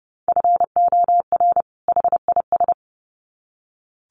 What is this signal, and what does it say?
Morse code, heard as a keyed tone, about 30 wpm, 700 Hz. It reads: FOR HIS
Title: Sets of 2 Words 30wpm